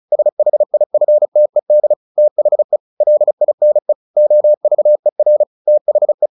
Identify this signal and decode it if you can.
Morse code, heard as a keyed tone, about 34 wpm, 600 Hz. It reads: SHIFTED THE LINE OVER THE
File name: the old man and the sea / the old man and the sea 1 35wpm